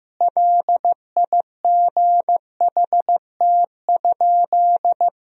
Morse code, heard as a keyed tone, about 16 wpm, 700 Hz. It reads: LIGHT?
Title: Sentences from Top 1000 Words - Review Part 1 - 15wpm